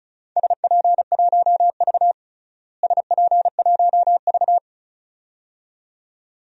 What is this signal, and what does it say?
Morse code, heard as a keyed tone, about 36 wpm, 700 Hz. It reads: SP1V SP1V
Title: DXCC-100 Call Signs - 4 Characters - Repeated 2x - 35wpm